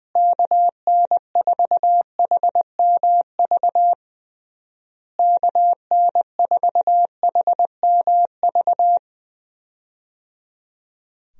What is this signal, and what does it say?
Morse code, heard as a keyed tone, about 21 wpm, 700 Hz. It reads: KN4HMV KN4HMV
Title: USA Call Signs - 6 Characters - Repeated 3x - 20wpm